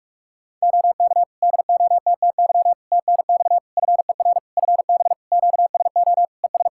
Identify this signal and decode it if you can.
Morse code, heard as a keyed tone, about 48 wpm, 700 Hz. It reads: OK DOTTY TNX FER FB QSO ES
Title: Practice QSO - Exchange 4 - Ending - 45wpm